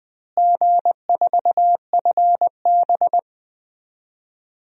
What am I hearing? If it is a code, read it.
Morse code, heard as a keyed tone, about 21 wpm, 700 Hz. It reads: G4FB
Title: DXCC-100 Call Signs - 4 Characters - Repeated 2x - 20wpm